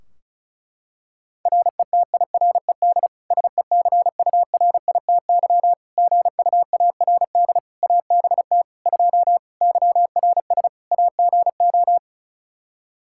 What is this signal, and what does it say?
Morse code, heard as a keyed tone, about 36 wpm, 700 Hz. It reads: RETIRED SECURITY GUARD ABT 2 YRS AGO
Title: Practice QSO - Exchange 3 - More Info - 35wpm